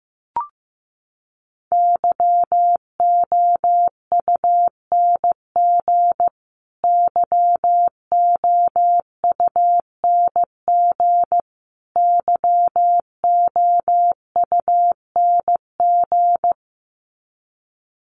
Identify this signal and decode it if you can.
Morse code, heard as a keyed tone, about 15 wpm, 700 Hz. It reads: YOUNG YOUNG YOUNG
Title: Top 500 Words Repeated 15wpm